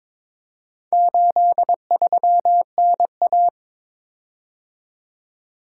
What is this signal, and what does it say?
Morse code, heard as a keyed tone, about 22 wpm, 700 Hz. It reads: 83NA
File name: practice / 22wpm / Four Letter-Number Combinations - 22wpm